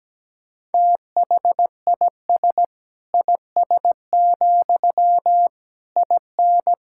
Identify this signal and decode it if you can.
Morse code, heard as a keyed tone, about 17 wpm, 700 Hz. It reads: THIS IS, IN
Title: Sentences from Top 1000 Words - Review Part 5 - 17wpm